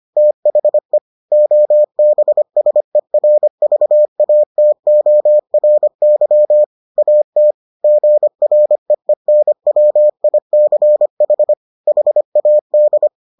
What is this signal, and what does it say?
Morse code, heard as a keyed tone, about 25 wpm, 600 Hz. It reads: THE OBSERVATORY AT GREENWICH HAD